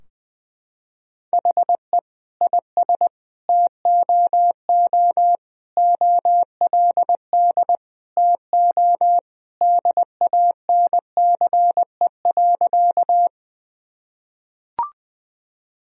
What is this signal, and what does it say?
Morse code, heard as a keyed tone, about 20 wpm, 700 Hz. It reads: HE IS TOO OLD TO DANCE.
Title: Sentences from Top 1000 Words - Review Part 4 - 20wpm